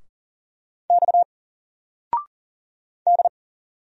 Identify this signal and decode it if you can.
Morse code, heard as a keyed tone, about 40 wpm, 700 Hz. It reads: X D
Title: Single Letter-Number 40wpm